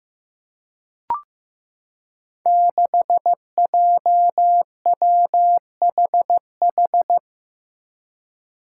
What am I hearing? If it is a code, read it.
Morse code, heard as a keyed tone, about 15 wpm, 700 Hz. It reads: 6JWHH
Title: Five Challenging Letter-Number Combinations 15wpm